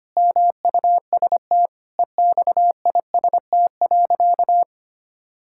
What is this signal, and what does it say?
Morse code, heard as a keyed tone, about 25 wpm, 700 Hz. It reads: MUST EXIST.
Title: Sentences from Top 1000 Words - Review Part 5 - 25wpm